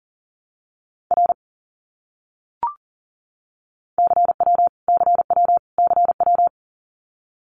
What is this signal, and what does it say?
Morse code, heard as a keyed tone, about 40 wpm, 700 Hz. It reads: R CW CW CW
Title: Abbreviations Repeated 3x - 40wpm